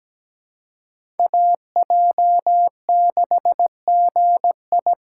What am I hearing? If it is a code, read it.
Morse code, heard as a keyed tone, about 17 wpm, 700 Hz. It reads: AJ6GI
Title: USA Call Signs - 5 Characters - 3x Character Spacing - 17wpm